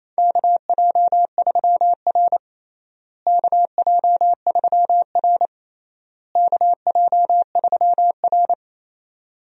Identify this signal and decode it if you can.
Morse code, heard as a keyed tone, about 27 wpm, 700 Hz. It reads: KJ3R KJ3R KJ3R
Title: USA Call Signs - 4 Characters - Repeated 3x - 28wpm